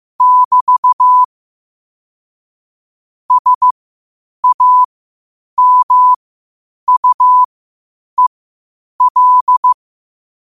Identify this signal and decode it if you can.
Morse code, heard as a keyed tone, about 15 wpm, spaced as for 8 wpm, 1000 Hz. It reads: <BT> SAMUEL
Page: New quotes every day in morse code at 8 Words per minute.